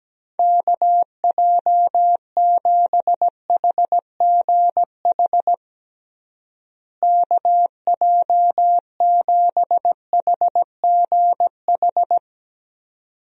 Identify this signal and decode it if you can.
Morse code, heard as a keyed tone, about 17 wpm, 700 Hz. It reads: KJ7HGH KJ7HGH
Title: USA Call Signs - 6 Characters - Repeated 2x - 17wpm